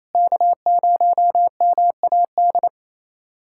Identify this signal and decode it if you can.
Morse code, heard as a keyed tone, about 28 wpm, 700 Hz. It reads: K0MAD
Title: USA Callsign DE Callsign - 5 Characters - 28wpm